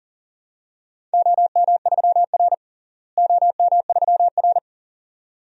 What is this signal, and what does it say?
Morse code, heard as a keyed tone, about 40 wpm, 700 Hz. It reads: OM3R OM3R
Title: DXCC-100 Call Signs - 4 Characters - Repeated 2x - 40wpm